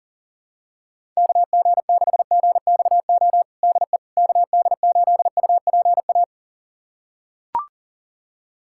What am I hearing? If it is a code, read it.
Morse code, heard as a keyed tone, about 37 wpm, 700 Hz. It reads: KG6GXO DE KD8UPA
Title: USA Callsign DE Callsign - 6 Characters - 40wpm